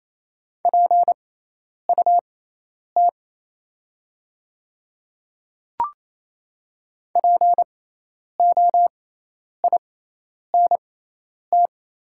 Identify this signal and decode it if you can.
Morse code, heard as a keyed tone, about 27 wpm, spaced as for 10 wpm, 690 Hz. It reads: PUT POINT
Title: 6x Character Spacing - Top 500 Words - 28wpm